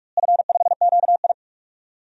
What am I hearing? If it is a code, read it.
Morse code, heard as a keyed tone, about 46 wpm, 700 Hz. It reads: F5QI
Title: DXCC-100 Call Signs - 4 Characters - Repeated 3x - 45wpm